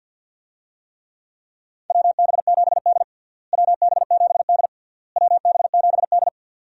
Morse code, heard as a keyed tone, about 48 wpm, 700 Hz. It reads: WB7D WB7D WB7D
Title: USA Call Signs - 4 Characters - Repeated 3x - 50wpm